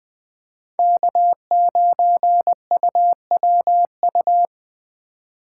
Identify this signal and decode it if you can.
Morse code, heard as a keyed tone, about 20 wpm, 700 Hz. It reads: K9UWU
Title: USA Call Signs - 5 Characters - 3x Character Spacing - 20wpm